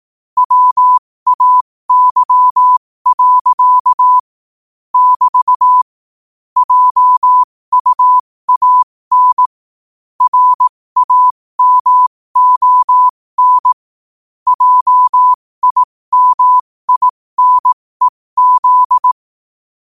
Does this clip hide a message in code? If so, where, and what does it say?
Yes: Morse code from 0.4 to 19.1 s, heard as a keyed tone, about 18 wpm, spaced as for 15 wpm, 1000 Hz. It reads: WAY. <BT> JUAN RAMON JIMINEZ